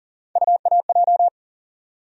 Morse code, heard as a keyed tone, about 40 wpm, 700 Hz. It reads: UAJ
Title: USA Call Sign Postfixes - 3 Letters 40wpm